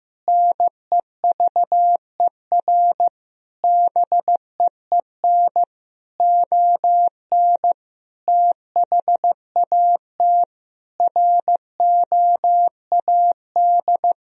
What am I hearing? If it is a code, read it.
Morse code, heard as a keyed tone, about 16 wpm, 700 Hz. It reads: NEVER BEEN ON THAT ROAD
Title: Sentences from Top 1000 Words - Review Part 2 - 15wpm